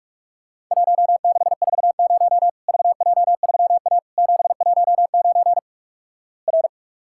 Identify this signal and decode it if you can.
Morse code, heard as a keyed tone, about 46 wpm, 700 Hz. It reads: 1640 VJ3A 719 R
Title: Worked All Europe DX Contest - QTC Exchange - 45wpm